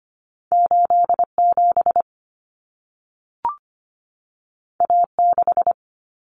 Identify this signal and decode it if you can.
Morse code, heard as a keyed tone, about 25 wpm, 700 Hz. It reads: 87 A6
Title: Common Cut Numbers - 2-Digit 25wpm